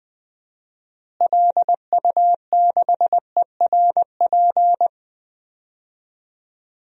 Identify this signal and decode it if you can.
Morse code, heard as a keyed tone, about 20 wpm, 690 Hz. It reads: LU6ERP
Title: DXCC-100 Call Signs - 6 Characters - Repeated 2x - 20wpm